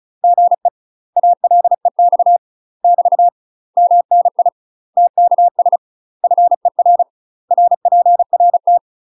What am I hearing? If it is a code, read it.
Morse code, heard as a keyed tone, about 36 wpm, 700 Hz. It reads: GE ALEX <BT> MNI TKS FER RPRT
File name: UK MASTER QSOs TRAINING / 35WPM / UK MASTER QSOs 0033